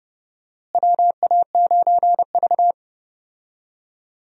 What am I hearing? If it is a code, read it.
Morse code, heard as a keyed tone, about 31 wpm, 700 Hz. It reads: WA9V
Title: USA Call Signs - 4 Characters - Repeated 2x - 30wpm